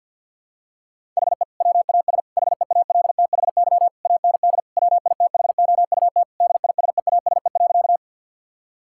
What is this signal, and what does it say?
Morse code, heard as a keyed tone, about 50 wpm, 700 Hz. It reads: HE WAS HEALTHY AND WITHOUT DISEASE.
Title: Sentences from Top 1000 Words - Review Part 6 - 50wpm